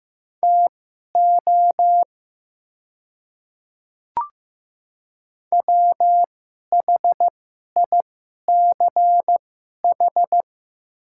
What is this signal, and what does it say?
Morse code, heard as a keyed tone, about 15 wpm, spaced as for 11 wpm, 700 Hz. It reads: TO WHICH
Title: 2x Character Spacing - Top 100 Words - 15wpm